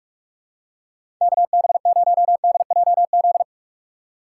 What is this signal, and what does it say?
Morse code, heard as a keyed tone, about 47 wpm, 700 Hz. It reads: KB0DJZ
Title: USA Call Signs - 6 Characters - 6x Character Spacing - 45wpm